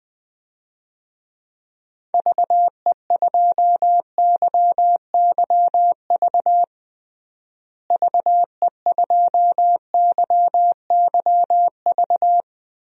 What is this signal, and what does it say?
Morse code, heard as a keyed tone, about 20 wpm, 700 Hz. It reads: VE2YYV VE2YYV
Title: DXCC-100 Call Signs - 6 Characters - Repeated 2x - 20wpm